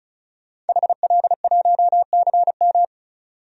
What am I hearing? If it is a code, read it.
Morse code, heard as a keyed tone, about 36 wpm, 700 Hz. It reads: HL1CM
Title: DXCC-100 Call Signs - 5 Characters - 35wpm